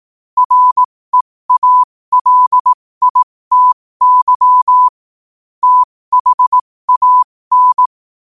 Morse code, heard as a keyed tone, about 18 wpm, spaced as for 15 wpm, 1000 Hz. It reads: REALITY THAN
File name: qotdfiles / 15wpm